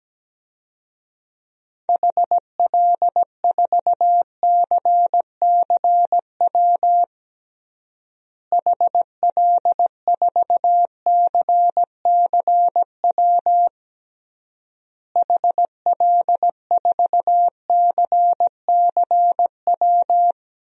Morse code, heard as a keyed tone, about 17 wpm, 700 Hz. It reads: HL4CCW HL4CCW HL4CCW
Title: DXCC-100 Call Signs - 6 Characters - Repeated 3x - 17wpm